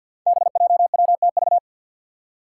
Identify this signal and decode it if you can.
Morse code, heard as a keyed tone, about 47 wpm, 700 Hz. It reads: BJWTV
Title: Five Challenging Letter-Number Combinations 50wpm